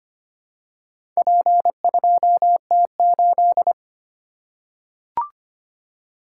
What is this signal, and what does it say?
Morse code, heard as a keyed tone, about 25 wpm, 700 Hz. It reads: P2T8
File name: practice / 25wpm / Four Letter-Number Combinations - 25wpm